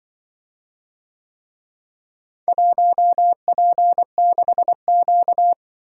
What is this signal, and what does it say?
Morse code, heard as a keyed tone, about 24 wpm, 700 Hz. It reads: 1P6Q
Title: Speed-Racing - Four Letter-Number Combinations - 20wpm